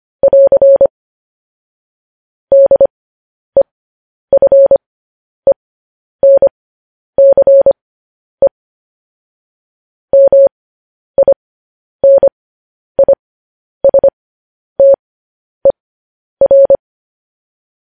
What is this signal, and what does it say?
Morse code, heard as a keyed tone, about 25 wpm, spaced as for 10 wpm, 550 Hz. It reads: <AR> DEFENCE MINISTER
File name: podcast / News headlines-1x-25x10